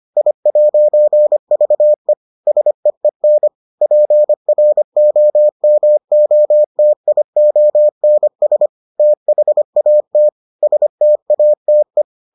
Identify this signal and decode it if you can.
Morse code, heard as a keyed tone, about 25 wpm, 600 Hz. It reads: I'VE SEEN PROMOTIONS THAT STATE